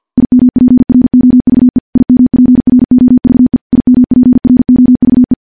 Morse code, heard as a keyed tone, about 50 wpm, 250 Hz. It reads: IMPROVE IMPROVE IMPROVE
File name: morse-training-50WPM